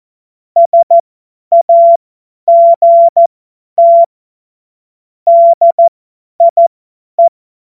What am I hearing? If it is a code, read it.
Morse code, heard as a keyed tone, about 14 wpm, spaced as for 10 wpm, 680 Hz. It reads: SAGT DIE
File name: Morsekurs / Witze / Witz 04. Juli 2023 50 F70